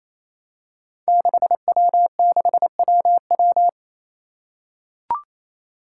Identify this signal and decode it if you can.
Morse code, heard as a keyed tone, about 27 wpm, 700 Hz. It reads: 6W6WW
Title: Five Challenging Letter-Number Combinations 28wpm